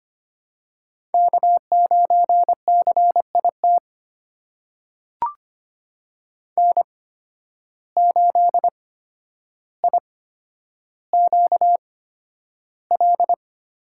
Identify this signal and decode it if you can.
Morse code, heard as a keyed tone, about 25 wpm, spaced as for 11 wpm, 700 Hz. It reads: K9CIT N8IQL
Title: USA Call Signs - 5 Characters - 8x Character Spacing - 25wpm